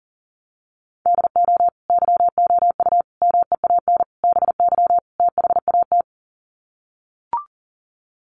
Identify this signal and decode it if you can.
Morse code, heard as a keyed tone, about 40 wpm, 700 Hz. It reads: DO YOU MEAN BY THAT
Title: Sets of 5 Words Encore2 40wpm